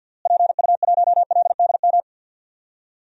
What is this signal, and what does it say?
Morse code, heard as a keyed tone, about 50 wpm, 700 Hz. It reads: PU1LDM
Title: DXCC-100 Call Signs - 6 Characters - Repeated 3x - 50wpm